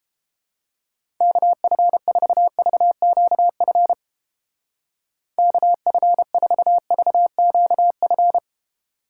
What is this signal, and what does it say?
Morse code, heard as a keyed tone, about 31 wpm, 700 Hz. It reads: KF4VQF KF4VQF